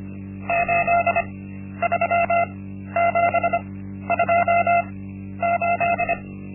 Morse code, heard as a keyed tone, about 25 wpm, spaced as for 12 wpm, 690 Hz. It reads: 83728
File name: Linksmakalnis-1